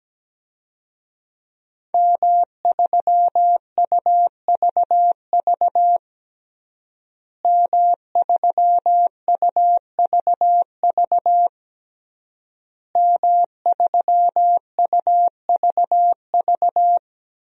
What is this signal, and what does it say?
Morse code, heard as a keyed tone, about 17 wpm, 700 Hz. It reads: M3UVV M3UVV M3UVV
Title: DXCC-100 Call Signs - 5 Characters - Repeated 3x - 17wpm